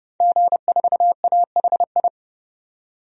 Morse code, heard as a keyed tone, about 30 wpm, 700 Hz. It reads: G4AHI
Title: DXCC-100 Call Signs - 5 Characters - 30wpm